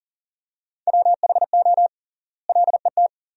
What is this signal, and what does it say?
Morse code, heard as a keyed tone, about 37 wpm, 700 Hz. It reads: WHO LET
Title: 3x Word-Spacing - Sentences from Top 300 Words - 40wpm